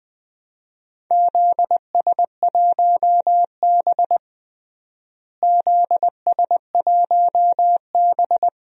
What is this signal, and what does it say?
Morse code, heard as a keyed tone, about 21 wpm, 700 Hz. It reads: ZS1B ZS1B
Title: DXCC-100 Call Signs - 4 Characters - Repeated 2x - 20wpm